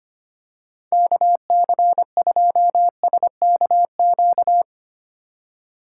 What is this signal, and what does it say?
Morse code, heard as a keyed tone, about 25 wpm, 700 Hz. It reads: KC2SKQ
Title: USA Call Signs - 6 Characters - 2x Character Spacing - 25wpm